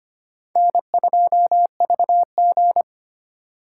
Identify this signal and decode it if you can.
Morse code, heard as a keyed tone, about 32 wpm, 700 Hz. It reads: N2VG